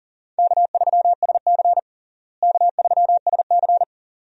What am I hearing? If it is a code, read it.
Morse code, heard as a keyed tone, about 40 wpm, 700 Hz. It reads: K3SC K3SC
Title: USA Call Signs - 4 Characters - Repeated 2x - 40wpm